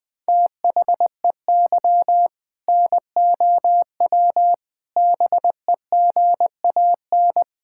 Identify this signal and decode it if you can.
Morse code, heard as a keyed tone, about 20 wpm, 700 Hz. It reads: THEY NOW BEGAN
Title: Sentences from Top 1000 Words - Review Part 5 - 20wpm